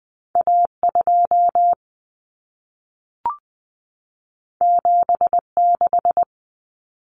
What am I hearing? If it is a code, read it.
Morse code, heard as a keyed tone, about 20 wpm, 700 Hz. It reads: A2 76
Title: Common Cut Numbers - 2-Digit 20wpm